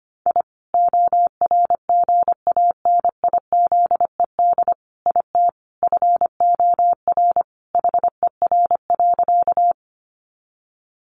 Morse code, heard as a keyed tone, about 25 wpm, 700 Hz. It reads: I ORGANIZED IT FOR HER.
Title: Sentences from Top 1300 Words 25wpm